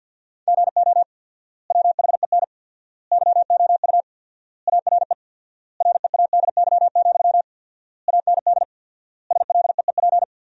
Morse code, heard as a keyed tone, about 48 wpm, 700 Hz. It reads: GO WHEN YOU ARE READY, AND SLEEP
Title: 4x Word-Spacing - Sentences from Top 500 Words - 50wpm